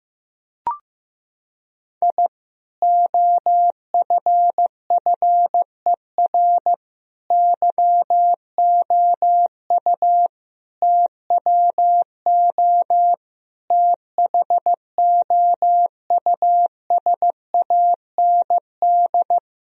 Morse code, heard as a keyed tone, about 15 wpm, 700 Hz. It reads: I OFFER YOU TWO THOUSAND
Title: Sentences from Top 1000 Words - Review Part 4 - 15wpm